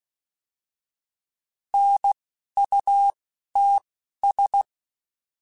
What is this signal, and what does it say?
Morse code, heard as a keyed tone, about 16 wpm, spaced as for 11 wpm, 800 Hz. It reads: NUTS